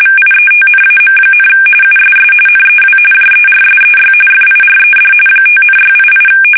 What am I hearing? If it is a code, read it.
RTTY, about 45.45 baud. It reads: VIETNAM, USS NEW JERSEY FIRED 5,688
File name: fsk850 / battleshipnj